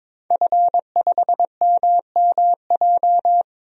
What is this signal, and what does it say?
Morse code, heard as a keyed tone, about 22 wpm, 700 Hz. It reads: F5MMJ
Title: DXCC-100 Call Signs - 5 Characters - 22wpm